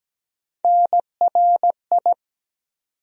Morse code, heard as a keyed tone, about 17 wpm, 700 Hz. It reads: NRI
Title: USA Call Sign Postfixes - 3 Letters 17wpm